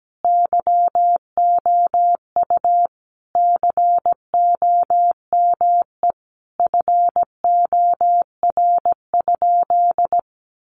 Morse code, heard as a keyed tone, about 17 wpm, 700 Hz. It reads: YOU COME FOR?
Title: Sentences from Top 100 Words 17wpm